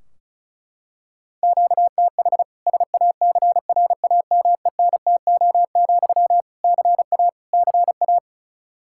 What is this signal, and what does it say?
Morse code, heard as a keyed tone, about 36 wpm, 700 Hz. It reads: QTH SACRAMENTO, CA CA
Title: Practice QSO - Exchange 1 - Beginning - 35wpm